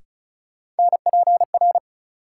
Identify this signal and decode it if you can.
Morse code, heard as a keyed tone, about 36 wpm, 700 Hz. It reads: NPR
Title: Sets of 5 Words - New Word Drill - 35wpm